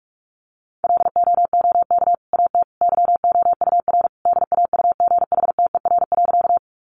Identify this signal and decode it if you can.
Morse code, heard as a keyed tone, about 44 wpm, 700 Hz. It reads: LOOK AT YOUR DAUGHTER.
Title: Sentences from Top 1500 Words 45wpm